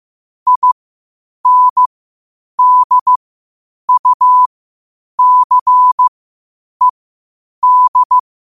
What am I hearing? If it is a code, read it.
Morse code, heard as a keyed tone, about 15 wpm, spaced as for 8 wpm, 1000 Hz. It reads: INDUCED
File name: qotdfiles / 8wpm